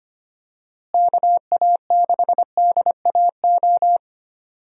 Morse code, heard as a keyed tone, about 25 wpm, 700 Hz. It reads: KA6DAO
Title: USA Call Signs - 6 Characters - Repeated 2x - 25wpm